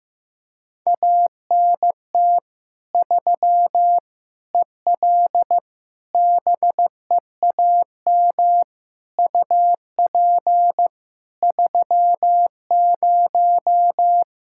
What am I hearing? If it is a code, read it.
Morse code, heard as a keyed tone, about 16 wpm, 700 Hz. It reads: ANT 3 EL BEAM UP 30
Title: Practice QSO - Exchange 2 - Info - 15wpm